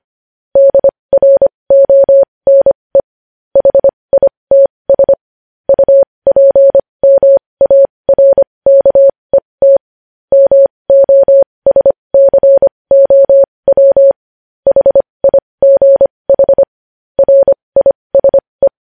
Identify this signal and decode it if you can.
Morse code, heard as a keyed tone, about 25 wpm, spaced as for 20 wpm, 550 Hz. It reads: DRONE HITS UPMARKET MOSCOW HIGH RISE
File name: podcast / News headlines-1x-25x20